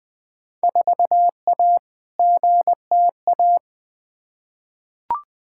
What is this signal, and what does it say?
Morse code, heard as a keyed tone, about 21 wpm, 700 Hz. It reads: 4A GTA
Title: Field Day Exchanges Encore - 20wpm